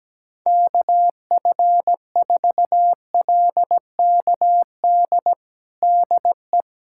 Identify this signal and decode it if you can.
Morse code, heard as a keyed tone, about 17 wpm, 700 Hz. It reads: KF4LKD DE
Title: USA Callsign DE Callsign - 6 Characters - 17wpm